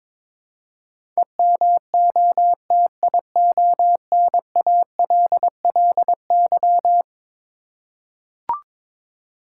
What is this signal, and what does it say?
Morse code, heard as a keyed tone, about 22 wpm, 700 Hz. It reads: EMOTIONALLY
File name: practice / 22wpm / Eleven Letter Words - 22wpm